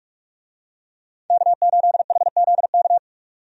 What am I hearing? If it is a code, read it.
Morse code, heard as a keyed tone, about 42 wpm, 700 Hz. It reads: K8HZK
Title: USA Call Signs - 5 Characters - 45wpm